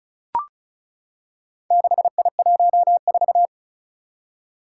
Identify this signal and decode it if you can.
Morse code, heard as a keyed tone, about 36 wpm, 700 Hz. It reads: 6I14
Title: Four Challenging Letter-Number Combinations 35wpm